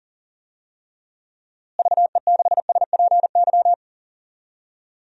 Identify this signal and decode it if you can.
Morse code, heard as a keyed tone, about 40 wpm, 700 Hz. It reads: VE6SPY
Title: DXCC-100 Call Signs - 6 Characters - Repeated 2x - 40wpm